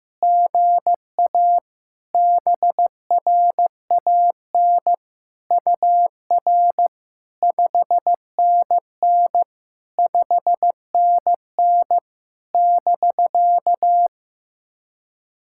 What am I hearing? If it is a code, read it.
Morse code, heard as a keyed tone, about 15 wpm, 700 Hz. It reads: GA BRAN UR 5NN 5NN <BK>
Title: SOTA QSO Practice 15wpm